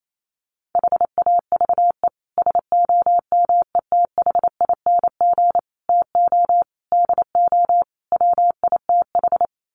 Morse code, heard as a keyed tone, about 28 wpm, 700 Hz. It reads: HAVE SOMETHING TO DO WITH
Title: Sets of 5 Words Encore 28wpm